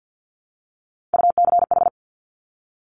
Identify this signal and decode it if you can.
Morse code, heard as a keyed tone, about 50 wpm, 700 Hz. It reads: UCH
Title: Top Three Letter Combinations in Words 50wpm